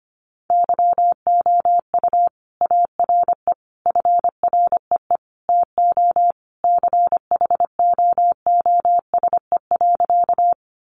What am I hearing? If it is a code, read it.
Morse code, heard as a keyed tone, about 25 wpm, 700 Hz. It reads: YOU ARE FREE TO CHOOSE.
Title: Sentences from Top 700 Words 25wpm